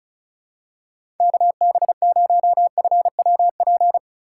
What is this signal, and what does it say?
Morse code, heard as a keyed tone, about 34 wpm, 700 Hz. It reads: KB0FWP
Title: USA Call Signs - 4x Character Spacing - 35wpm